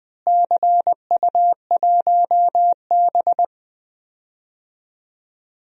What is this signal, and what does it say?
Morse code, heard as a keyed tone, about 20 wpm, 700 Hz. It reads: CU1B
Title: DXCC-100 Call Signs - 4 Characters - Repeated 3x - 20wpm